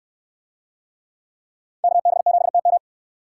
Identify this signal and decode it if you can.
Morse code, heard as a keyed tone, about 69 wpm, 700 Hz. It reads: KB7TM